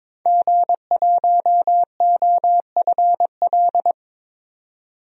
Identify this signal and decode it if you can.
Morse code, heard as a keyed tone, about 22 wpm, 700 Hz. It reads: G1OFL
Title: DXCC-100 Call Signs - 5 Characters - 22wpm